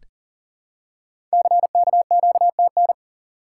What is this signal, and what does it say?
Morse code, heard as a keyed tone, about 42 wpm, 700 Hz. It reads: CKQTN
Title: Code Groups - 5 Letters - 40wpm